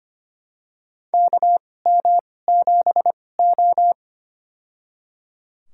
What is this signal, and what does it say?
Morse code, heard as a keyed tone, about 25 wpm, spaced as for 18 wpm, 700 Hz. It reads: KM7O
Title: USA Call Signs - 4 Characters - 2x Character Spacing - 25wpm